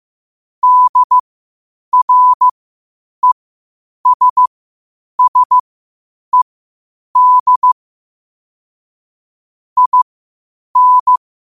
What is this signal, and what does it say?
Morse code, heard as a keyed tone, about 15 wpm, spaced as for 8 wpm, 1000 Hz. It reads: DRESSED IN
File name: qotdfiles / 8wpm